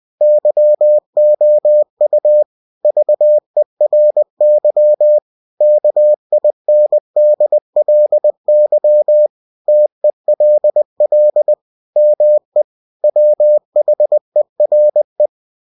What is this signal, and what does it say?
Morse code, heard as a keyed tone, about 20 wpm, 600 Hz. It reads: YOU VERY KINDLY TELL ME WHERE
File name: cw-books / winnie-the-pooh / winnie-20wpm-page-0014